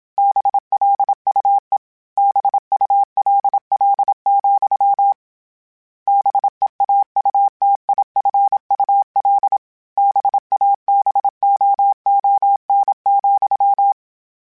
Morse code, heard as a keyed tone, about 27 wpm, 800 Hz. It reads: BLUE BULL, BEAUTIFUL BABOON,